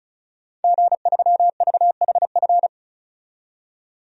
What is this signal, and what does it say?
Morse code, heard as a keyed tone, about 34 wpm, 700 Hz. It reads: G3VHF
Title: DXCC-100 Call Signs - 5 Characters - 35wpm